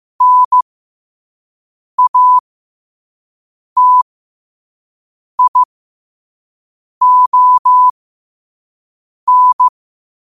Morse code, heard as a keyed tone, about 15 wpm, spaced as for 5 wpm, 1000 Hz. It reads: NATION
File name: qotdfiles / 5wpm